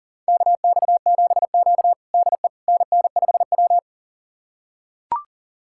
Morse code, heard as a keyed tone, about 40 wpm, 700 Hz. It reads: KX7Q DE NN5W
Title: USA Callsign DE Callsign - 4 Characters - 40wpm